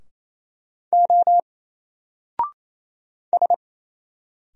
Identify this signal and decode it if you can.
Morse code, heard as a keyed tone, about 28 wpm, 700 Hz. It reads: O S
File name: Single-Letter-Number-28wpm